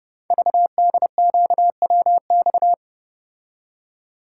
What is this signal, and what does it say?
Morse code, heard as a keyed tone, about 29 wpm, 700 Hz. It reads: VDQWX
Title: Code Groups - 5 Letter:Numbers - 30wpm